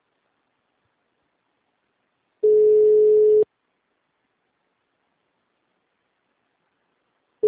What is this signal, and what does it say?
Ringback tone (Europe, 425 Hz).